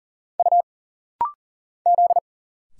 Morse code, heard as a keyed tone, about 40 wpm, 700 Hz. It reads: U Z
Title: Single Letters - Rapid-Fire 40wpm